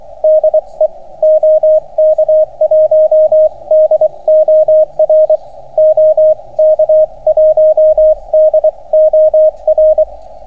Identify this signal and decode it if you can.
Morse code, heard as a keyed tone, about 24 wpm, spaced as for 21 wpm, 620 Hz, with radio noise. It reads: DE OK1DOR OK1DOR